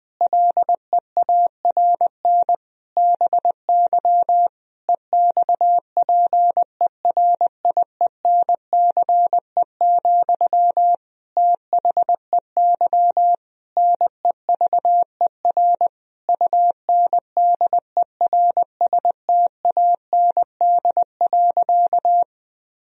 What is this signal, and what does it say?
Morse code, heard as a keyed tone, about 21 wpm, 700 Hz. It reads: LEARN BY EXPERIENCE, THEY NEVER UNDERSTAND.
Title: Sentences from Top 1000 Words - Review Part 3 - 20wpm